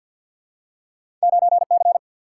Morse code, heard as a keyed tone, about 53 wpm, 700 Hz. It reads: 9 C
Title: Single-Letter Single-Number 50wpm